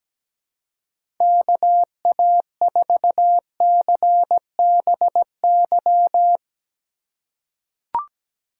Morse code, heard as a keyed tone, about 17 wpm, 700 Hz. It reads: KA4CBY
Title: USA Call Signs - 6 Characters - 3x Character Spacing - 17wpm